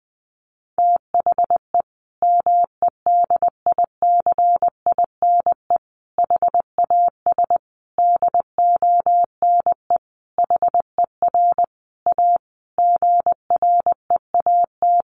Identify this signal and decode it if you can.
Morse code, heard as a keyed tone, about 20 wpm, 700 Hz. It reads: THE MEDICINE HAS DONE HER A GREAT
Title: Sentences from Top 900 Words 20wpm